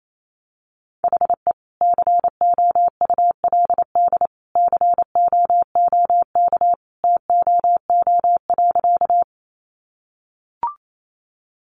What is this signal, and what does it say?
Morse code, heard as a keyed tone, about 28 wpm, 700 Hz. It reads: HE COULD COOK TOO.
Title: Sentences from Top 700 Words 28wpm